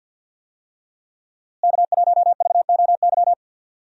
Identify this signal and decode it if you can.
Morse code, heard as a keyed tone, about 52 wpm, 700 Hz. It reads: K1VOY
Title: USA Call Signs - 5 Characters - Repeated 2x - 50wpm